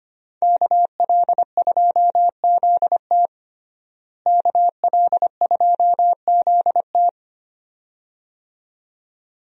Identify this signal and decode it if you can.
Morse code, heard as a keyed tone, about 25 wpm, 700 Hz. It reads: KL2ZT KL2ZT
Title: USA Call Signs - 5 Characters - Repeated 2x - 25wpm